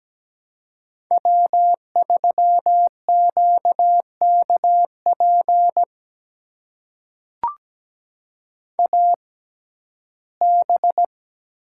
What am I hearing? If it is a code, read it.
Morse code, heard as a keyed tone, about 17 wpm, spaced as for 12 wpm, 700 Hz. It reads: W3QKP AB
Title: USA Call Signs - 5 Characters - 6x Character Spacing - 17wpm